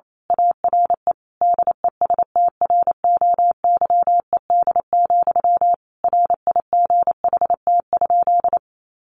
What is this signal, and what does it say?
Morse code, heard as a keyed tone, about 28 wpm, 700 Hz. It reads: ARE DESTROYED, RIGHT?
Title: A Sound Of Thunder 28wpm